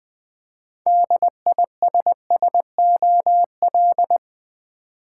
Morse code, heard as a keyed tone, about 20 wpm, 700 Hz. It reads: DISSOL
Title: Build A Word - 2 Letters At A Time - 8-Letter Words - 20wpm